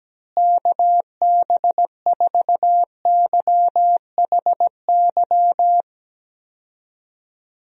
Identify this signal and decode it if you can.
Morse code, heard as a keyed tone, about 17 wpm, 700 Hz. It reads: KB4YHY
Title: USA Call Signs - 6 Characters - Repeated 3x - 17wpm